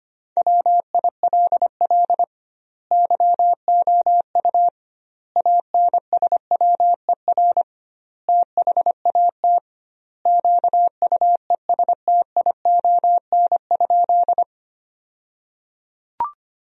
Morse code, heard as a keyed tone, about 25 wpm, 700 Hz. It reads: WILL YOU ANSWER THAT QUESTION?
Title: 2x Word-Spacing - Sentences from Top 400 Words - 25wpm